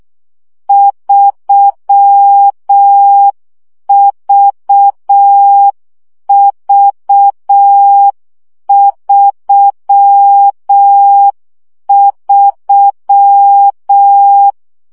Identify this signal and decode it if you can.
Morse code, heard as a keyed tone, about 6 wpm, 800 Hz. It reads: 3VV33
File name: CW gyakorlas v 3 5 baud